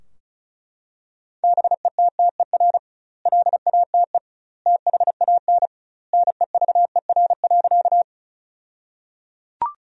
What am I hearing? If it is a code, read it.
Morse code, heard as a keyed tone, about 36 wpm, 700 Hz. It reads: BETTER LATE THAN NEVER.
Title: 2x Word-Spacing - Sentences from Top 400 Words - 35wpm